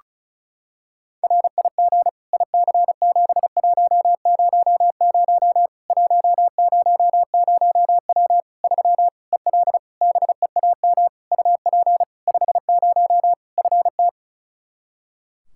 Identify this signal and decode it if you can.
Morse code, heard as a keyed tone, about 34 wpm, 700 Hz. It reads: RIG IC7100 100W 3 EL BEAM UP 50 FT
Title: Practice QSO - Exchange 2 - Info - 35wpm